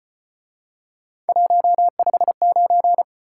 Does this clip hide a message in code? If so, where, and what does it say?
Yes: Morse code from 1.3 to 3.0 s, heard as a keyed tone, about 33 wpm, 700 Hz. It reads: 159